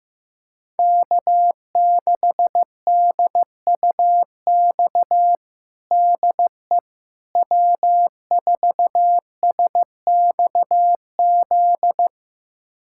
Morse code, heard as a keyed tone, about 15 wpm, 700 Hz. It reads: K6DUX DE W4SXZ
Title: USA Callsign DE Callsign - 5 Characters - 15wpm